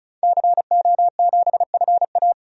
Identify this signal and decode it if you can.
Morse code, heard as a keyed tone, about 36 wpm, 700 Hz. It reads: CO7FA
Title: DXCC-100 Call Signs - 5 Characters - 35wpm